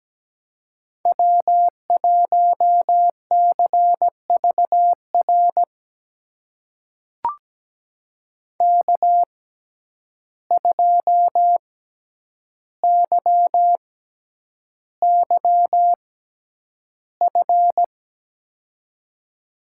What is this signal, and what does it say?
Morse code, heard as a keyed tone, about 17 wpm, spaced as for 9 wpm, 700 Hz. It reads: W1CVR K2YYF
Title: USA Call Signs - 5 Characters - 6x Character Spacing - 17wpm